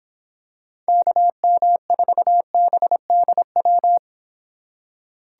Morse code, heard as a keyed tone, about 26 wpm, 700 Hz. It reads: KM4BDW